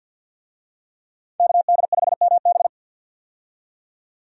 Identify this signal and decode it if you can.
Morse code, heard as a keyed tone, about 53 wpm, 700 Hz. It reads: KD5MB
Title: USA Call Signs - 5 Characters - 50wpm